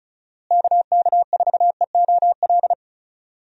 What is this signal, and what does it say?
Morse code, heard as a keyed tone, about 34 wpm, 700 Hz. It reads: KK4EOL
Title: USA Call Signs - 6 Characters - Repeated 3x - 35wpm